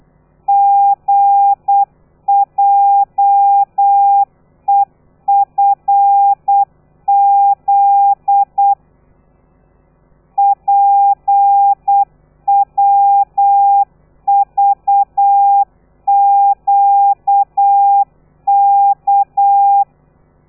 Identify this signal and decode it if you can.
Morse code, heard as a keyed tone, about 8 wpm, 800 Hz. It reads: GJEFZ PWVQK
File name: telegrafia / i2tfj / Lez13 35